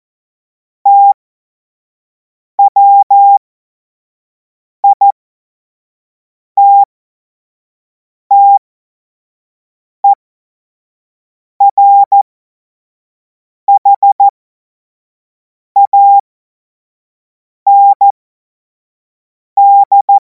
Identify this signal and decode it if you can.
Morse code, heard as a keyed tone, about 14 wpm, spaced as for 5 wpm, 800 Hz. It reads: TWITTERHAND
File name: Morsekurs / Witze / Witz 30. Juni 2026 25 F70